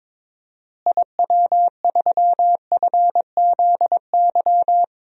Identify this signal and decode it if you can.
Morse code, heard as a keyed tone, about 22 wpm, 700 Hz. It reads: IW3FZY
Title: DXCC-100 Call Signs - 6 Characters - Repeated 3x - 22wpm